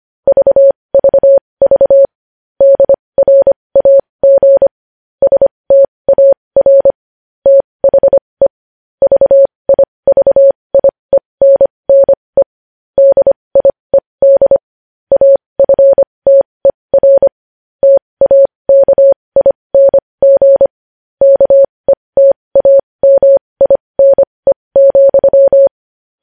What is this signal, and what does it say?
Morse code, heard as a keyed tone, about 25 wpm, spaced as for 20 wpm, 550 Hz. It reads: VVV DRAG STAR THE VIVIENNE DIED AFTER TAKING KETAMINE,
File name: podcast / News headlines-1x-25x20